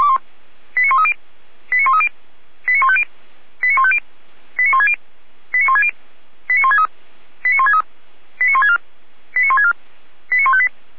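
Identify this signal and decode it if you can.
Five-tone selective call (ZVEI), heard as quick runs of five tones, tones of about 70 ms. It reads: 89130 89130 89150 89150 89160 89160 89163 89163 89164 89164 89158
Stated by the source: ZVEI five-tone call received by SDR.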